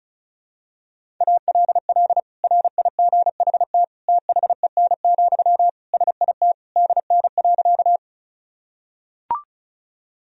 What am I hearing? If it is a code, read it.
Morse code, heard as a keyed tone, about 36 wpm, 700 Hz. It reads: ALL RIGHT THEN, SIT DN.
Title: Sentences from Top 500 Words and Abbreviations 35wpm